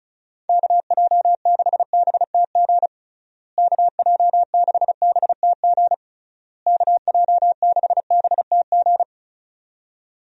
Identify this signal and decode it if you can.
Morse code, heard as a keyed tone, about 36 wpm, 700 Hz. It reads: KJ6BTG KJ6BTG KJ6BTG
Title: USA Call Signs - 6 Characters - Repeated 3x - 35wpm